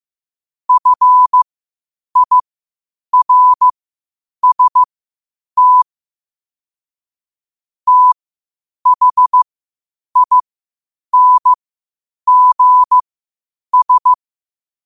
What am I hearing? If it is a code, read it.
Morse code, heard as a keyed tone, about 15 wpm, spaced as for 8 wpm, 1000 Hz. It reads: FIRST THINGS
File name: qotdfiles / 8wpm